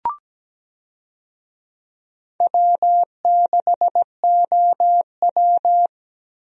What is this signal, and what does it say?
Morse code, heard as a keyed tone, about 17 wpm, 700 Hz. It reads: W6OW
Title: USA Call Signs - 4 Characters - Repeated 3x - 17wpm